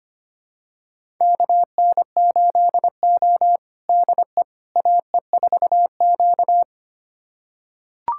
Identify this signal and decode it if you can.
Morse code, heard as a keyed tone, about 25 wpm, 700 Hz. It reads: KN8O DE AE4Q
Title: USA Callsign DE Callsign - 4 Characters - 25wpm